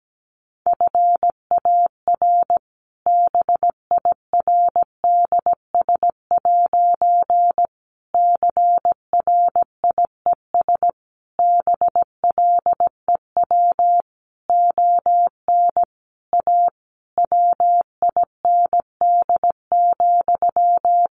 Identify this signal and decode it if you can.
Morse code, heard as a keyed tone, about 17 wpm, 700 Hz. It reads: FAR BIRDS' CRIES BLEW ON A WIND,
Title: A Sound Of Thunder 17wpm